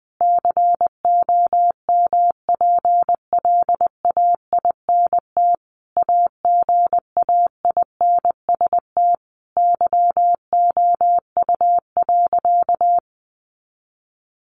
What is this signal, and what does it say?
Morse code, heard as a keyed tone, about 20 wpm, 700 Hz. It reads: COMPLAINT AGAINST YOU.
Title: Sentences from Top 1400 Words 20wpm